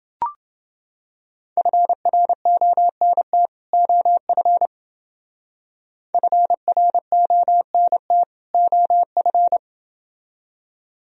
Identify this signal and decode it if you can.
Morse code, heard as a keyed tone, about 29 wpm, 700 Hz. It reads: FRONT OF FRONT OF